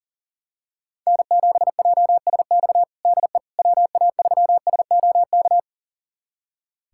Morse code, heard as a keyed tone, about 40 wpm, 700 Hz. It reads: N7JSX DE WA3SOK
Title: USA Callsign DE Callsign 40wpm